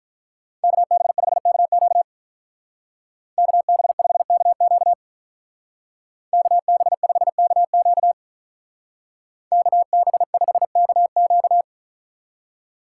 Morse code, heard as a keyed tone, about 43 wpm, 700 Hz. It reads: KB5KQ KB5KQ KB5KQ KB5KQ
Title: Speed Racing - USA 2x2 Call Signs 35wpm